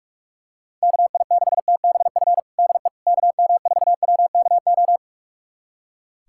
Morse code, heard as a keyed tone, about 42 wpm, 700 Hz. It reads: KI6TBF DE KM4WKO
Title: USA Callsign DE Callsign 45wpm